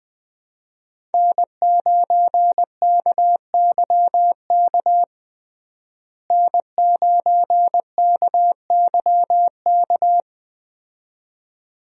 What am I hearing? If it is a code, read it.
Morse code, heard as a keyed tone, about 20 wpm, 700 Hz. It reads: N9KYK N9KYK
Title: USA Call Signs - 5 Characters - Repeated 2x - 20wpm